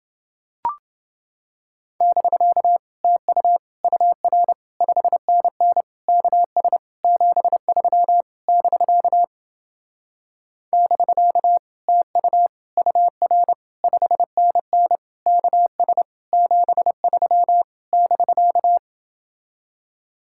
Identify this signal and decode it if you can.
Morse code, heard as a keyed tone, about 28 wpm, 700 Hz. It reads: <BK> TU UR 5NN KS 73 <BK> <BK> TU UR 5NN KS 73 <BK>